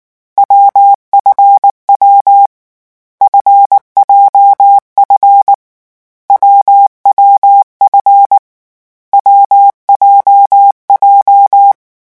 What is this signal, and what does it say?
Morse code, heard as a keyed tone, about 19 wpm, 800 Hz. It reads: WFW FJF WWF WJJ
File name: morse mp3 / lecon7-vitesse 18